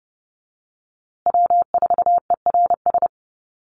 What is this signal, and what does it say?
Morse code, heard as a keyed tone, about 30 wpm, 700 Hz. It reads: W4ERS
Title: USA Call Signs Repeated 2x 30wpm